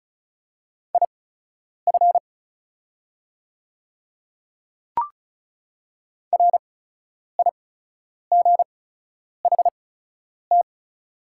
Morse code, heard as a keyed tone, about 36 wpm, spaced as for 10 wpm, 700 Hz. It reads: IF RIGHT
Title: 8x Character Spacing - Top 200 Words - 35wpm